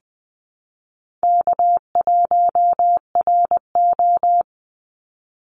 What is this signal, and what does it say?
Morse code, heard as a keyed tone, about 20 wpm, 700 Hz. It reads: K1RO
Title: Easy Call Signs - 20wpm